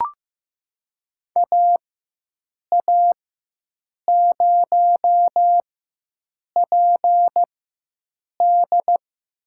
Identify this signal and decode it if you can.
Morse code, heard as a keyed tone, about 16 wpm, spaced as for 7 wpm, 700 Hz. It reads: AA0PD
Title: USA Call Signs - 5 Characters - 4x Character Spacing - 15wpm